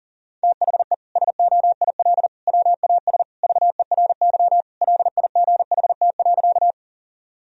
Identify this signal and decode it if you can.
Morse code, heard as a keyed tone, about 40 wpm, 700 Hz. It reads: THE SOIL WAS VERY LIGHT.
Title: Sentences from Top 1000 Words - Review Part 4 - 40wpm